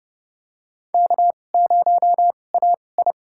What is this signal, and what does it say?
Morse code, heard as a keyed tone, about 30 wpm, spaced as for 22 wpm, 700 Hz. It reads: K0AI
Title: USA Call Signs - 4 Characters - 2x Character Spacing - 30wpm